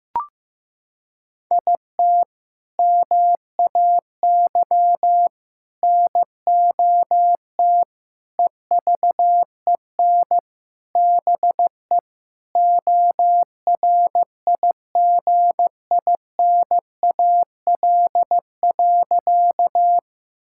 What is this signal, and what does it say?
Morse code, heard as a keyed tone, about 15 wpm, 700 Hz. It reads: IT MAY NOT EVEN BE ORIGINAL.
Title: Sentences from Top 1000 Words - Review Part 6 - 15wpm